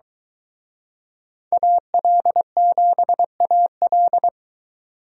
Morse code, heard as a keyed tone, about 23 wpm, 700 Hz. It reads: AL7AL